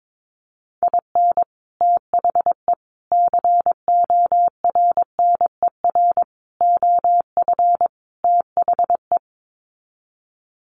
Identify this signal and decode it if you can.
Morse code, heard as a keyed tone, about 22 wpm, 700 Hz. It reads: IN THE CORNER OF THE
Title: Sets of 5 Words Encore 22wpm